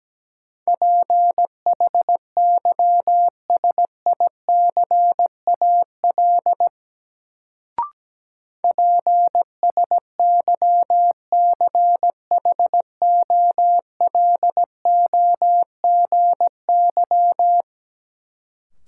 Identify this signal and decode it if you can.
Morse code, heard as a keyed tone, about 17 wpm, 700 Hz. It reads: PHYSICAL PSYCHOLOGY
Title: Sentences from Top 1400 Words - New Word Drill Alt - 17wpm